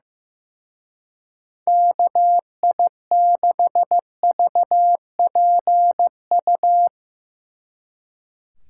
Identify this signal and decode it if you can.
Morse code, heard as a keyed tone, about 15 wpm, 700 Hz. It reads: KI6VPU
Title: Speed Racing - USA 2x3 Call Signs 15wpm